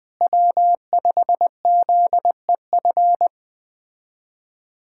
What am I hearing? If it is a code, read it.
Morse code, heard as a keyed tone, about 20 wpm, 700 Hz. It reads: W5ZEF
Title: USA Callsign DE Callsign - 5 Characters - 20wpm